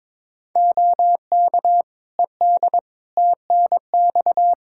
Morse code, heard as a keyed tone, about 22 wpm, 700 Hz. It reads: OK ED TNX
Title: K1USN SST QSO Practice 22wpm